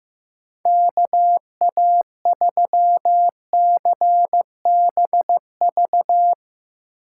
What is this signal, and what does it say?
Morse code, heard as a keyed tone, about 15 wpm, 700 Hz. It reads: KA3CBV
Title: Speed Racing - USA 2x3 Call Signs 15wpm